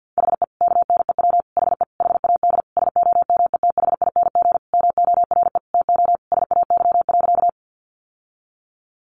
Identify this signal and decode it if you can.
Morse code, heard as a keyed tone, about 48 wpm, 700 Hz. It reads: HE KNEW HE HAD SOMETHING MORE TO SAY.
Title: Sentences from Top 600 Words 50wpm